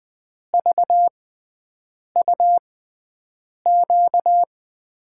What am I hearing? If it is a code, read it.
Morse code, heard as a keyed tone, about 20 wpm, spaced as for 7 wpm, 700 Hz. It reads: VUQ
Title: USA Call Signs - 5 Characters - 6x Character Spacing - 20wpm